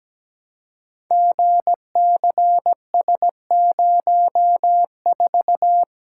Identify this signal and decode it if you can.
Morse code, heard as a keyed tone, about 17 wpm, 700 Hz. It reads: GCS04
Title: Code Groups - 5 Letter:Numbers - 17wpm